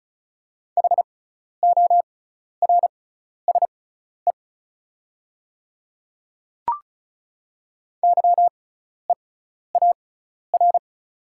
Morse code, heard as a keyed tone, about 36 wpm, spaced as for 12 wpm, 700 Hz. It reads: HORSE YEAR
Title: 6x Character Spacing - Top 300 Words - 35wpm